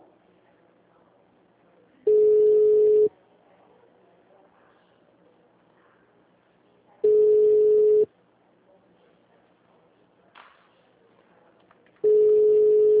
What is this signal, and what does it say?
Ringback tone (Europe, 425 Hz).